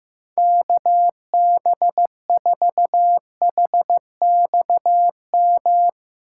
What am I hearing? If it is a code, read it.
Morse code, heard as a keyed tone, about 16 wpm, 700 Hz. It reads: KB4HXM
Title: Speed Racing - USA 2x3 Call Signs 15wpm